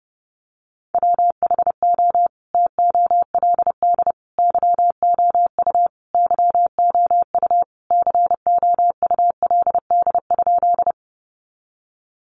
Morse code, heard as a keyed tone, about 30 wpm, 700 Hz. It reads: WHO TOLD YOU YOU COULD?
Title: Sentences from Top 400 Words 30wpm